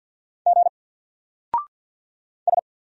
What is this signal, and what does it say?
Morse code, heard as a keyed tone, about 48 wpm, 700 Hz. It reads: G S
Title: Single Character (Letter:Number:Punctuation:Prosign) Alt 50wpm